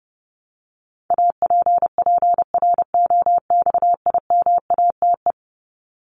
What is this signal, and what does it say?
Morse code, heard as a keyed tone, about 30 wpm, 700 Hz. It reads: APPROXIMATE
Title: LY Suffix - Top 100 Words with Base Words 30wpm